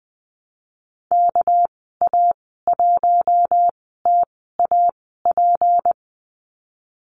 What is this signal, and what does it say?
Morse code, heard as a keyed tone, about 20 wpm, spaced as for 14 wpm, 700 Hz. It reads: KA1TAP
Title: 207 Lesson 33g - Letter j - Call Signs 20wpm